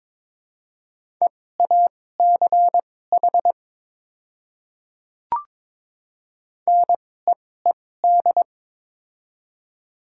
Morse code, heard as a keyed tone, about 22 wpm, spaced as for 16 wpm, 700 Hz. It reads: EACH NEED
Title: 2x Character Spacing - Top 300 Words - 22wpm